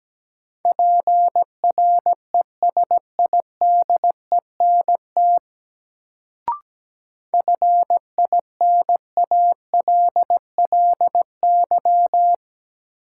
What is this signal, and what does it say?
Morse code, heard as a keyed tone, about 17 wpm, 700 Hz. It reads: PRESIDENT FINALLY
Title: Sentences from Top 1000 Words - New Word Drill - 17wpm